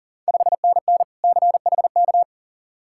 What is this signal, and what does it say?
Morse code, heard as a keyed tone, about 40 wpm, 690 Hz. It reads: 5NN CHK
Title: Ontario QSO Party Exchange 40wpm